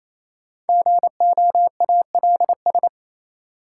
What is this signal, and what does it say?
Morse code, heard as a keyed tone, about 28 wpm, 700 Hz. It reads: GOALS
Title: Sentences from Top 1000 Words - Review Part 6 - New Word Drill - 28wpm